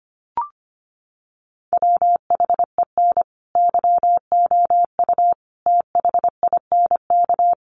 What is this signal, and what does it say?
Morse code, heard as a keyed tone, about 25 wpm, 700 Hz. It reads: WHEN YOU THINK
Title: Sets of 5 Words Encore2 25wpm